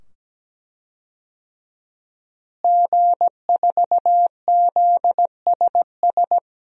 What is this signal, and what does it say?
Morse code, heard as a keyed tone, about 17 wpm, 700 Hz. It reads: G4ZSS
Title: DXCC-100 Call Signs - 5 Characters - Repeated 2x - 17wpm